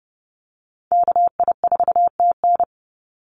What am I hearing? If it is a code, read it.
Morse code, heard as a keyed tone, about 30 wpm, 700 Hz. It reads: KI4TN
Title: NA SOTA Chaser Callsigns 30wpm